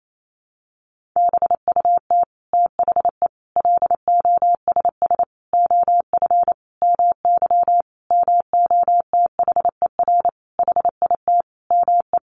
Morse code, heard as a keyed tone, about 28 wpm, 700 Hz. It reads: BUT THE LOSS OF MY MOTHER HIT ME
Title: Sentences from Top 1400 Words 28wpm